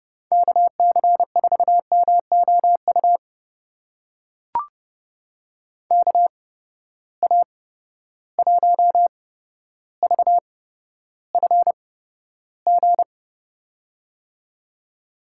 Morse code, heard as a keyed tone, about 32 wpm, spaced as for 13 wpm, 700 Hz. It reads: KC4MOU KA1VFG
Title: USA Call Signs - 6 Characters - 8x Character Spacing - 30wpm